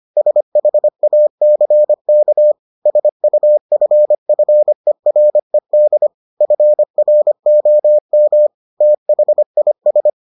Morse code, heard as a keyed tone, about 25 wpm, 600 Hz. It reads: SHACK SUFFERED FROM THIS